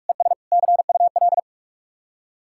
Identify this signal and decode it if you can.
Morse code, heard as a keyed tone, about 44 wpm, 700 Hz. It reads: ES CUL